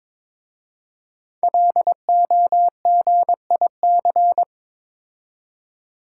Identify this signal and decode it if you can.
Morse code, heard as a keyed tone, about 22 wpm, 700 Hz. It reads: LOGIC
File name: 5 Letter Words - First 3 Letters Given - 22wpm